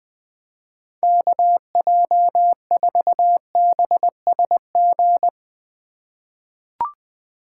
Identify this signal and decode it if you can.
Morse code, heard as a keyed tone, about 21 wpm, 700 Hz. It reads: KJ4BSG
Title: USA Call Signs - 6 Characters - 4x Character Spacing - 20wpm